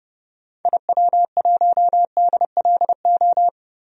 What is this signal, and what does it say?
Morse code, heard as a keyed tone, about 32 wpm, 700 Hz. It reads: IW1DLO
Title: DXCC-100 Call Signs - 6 Characters - Repeated 2x - 30wpm